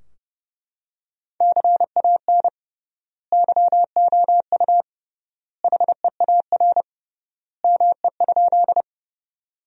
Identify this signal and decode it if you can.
Morse code, heard as a keyed tone, about 30 wpm, 700 Hz. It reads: CAN YOU HEAR ME?
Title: 3x Word-Spacing - Sentences from Top 300 Words - 30wpm